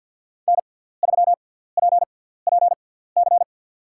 Morse code, heard as a keyed tone, about 53 wpm, spaced as for 17 wpm, 700 Hz. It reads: N3PPC
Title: USA Call Signs - 5 Characters - 6x Character Spacing - 50wpm